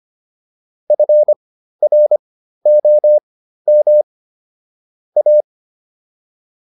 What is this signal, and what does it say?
Morse code, heard as a keyed tone, about 25 wpm, spaced as for 13 wpm, 600 Hz. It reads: FROM A